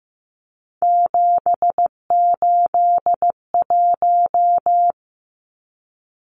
Morse code, heard as a keyed tone, about 15 wpm, 700 Hz. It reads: 781
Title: Numbers - 3-Digit 15wpm